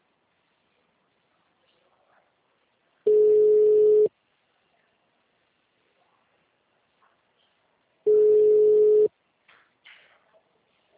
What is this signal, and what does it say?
Ringback tone (Europe, 425 Hz).